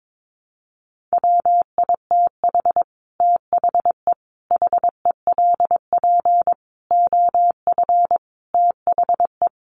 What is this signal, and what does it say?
Morse code, heard as a keyed tone, about 22 wpm, 700 Hz. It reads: WITH THE HELP OF THE
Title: Sets of 5 Words Encore 22wpm